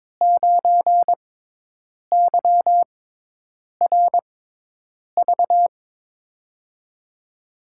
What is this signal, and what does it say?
Morse code, heard as a keyed tone, about 22 wpm, spaced as for 8 wpm, 700 Hz. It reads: 9YRV
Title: USA Call Signs - 6 Characters - 6x Character Spacing - 22wpm